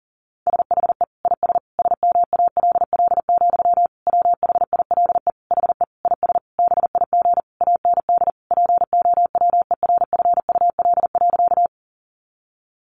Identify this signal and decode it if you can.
Morse code, heard as a keyed tone, about 40 wpm, 700 Hz. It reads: SHE IS SMALL, WHILE HE IS BIG AND POWERFUL.
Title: Sentences from Top 1300 Words 40wpm